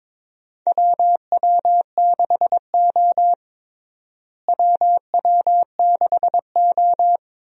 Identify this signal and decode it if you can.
Morse code, heard as a keyed tone, about 21 wpm, 700 Hz. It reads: WW6O WW6O
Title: USA Call Signs - 4 Characters - Repeated 3x - 22wpm